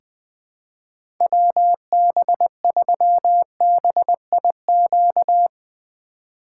Morse code, heard as a keyed tone, about 20 wpm, 700 Hz. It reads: WB3BIQ
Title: K1USN SST QSO Practice - 20wpm